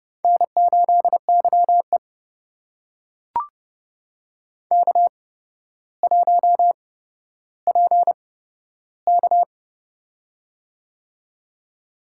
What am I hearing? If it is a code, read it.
Morse code, heard as a keyed tone, about 32 wpm, spaced as for 13 wpm, 700 Hz. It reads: N8YE K1PK
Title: USA Call Signs - 4 Characters - 8x Character Spacing - 30wpm